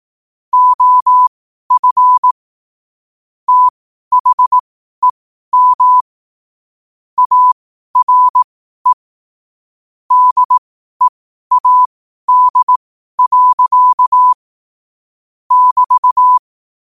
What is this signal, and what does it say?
Morse code, heard as a keyed tone, about 18 wpm, spaced as for 12 wpm, 1000 Hz. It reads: OF THEM ARE DEAD. <BT>
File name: qotdfiles / 12wpm